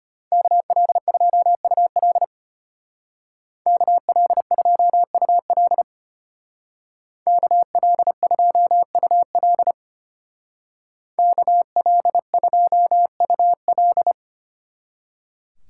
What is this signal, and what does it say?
Morse code, heard as a keyed tone, about 31 wpm, 700 Hz. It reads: KL2UL KL2UL KL2UL KL2UL
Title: Speed Racing - USA Call Signs 25wpm